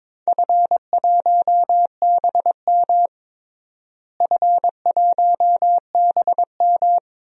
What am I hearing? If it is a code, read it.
Morse code, heard as a keyed tone, about 22 wpm, 700 Hz. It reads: F1BM F1BM
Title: DXCC-100 Call Signs - 4 Characters - Repeated 3x - 22wpm